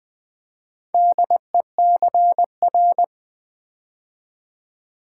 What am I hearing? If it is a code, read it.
Morse code, heard as a keyed tone, about 20 wpm, 700 Hz. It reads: DECR
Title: Build A Word - 4 Letters At A Time - 8-Letter Words - 20wpm